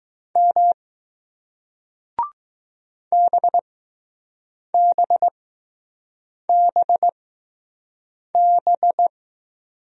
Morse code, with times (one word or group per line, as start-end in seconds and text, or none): 0.4-0.7 M
3.1-9.1 BBBB